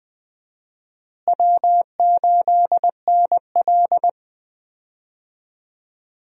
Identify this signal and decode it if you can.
Morse code, heard as a keyed tone, about 21 wpm, 700 Hz. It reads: W8NL
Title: USA Call Signs - 4 Characters - Repeated 3x - 20wpm